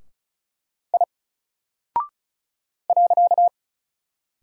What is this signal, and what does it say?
Morse code, heard as a keyed tone, about 36 wpm, 700 Hz. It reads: I .
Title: Single Character (Letter:Number:Punctuation:Prosign) Alt 35wpm